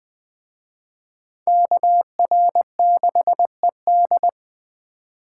Morse code, heard as a keyed tone, about 21 wpm, 700 Hz. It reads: KR6ED